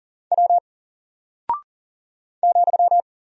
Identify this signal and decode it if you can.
Morse code, heard as a keyed tone, about 40 wpm, 700 Hz. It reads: W ,
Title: Single Character (Letter:Number:Punctuation:Prosign) Alt 40wpm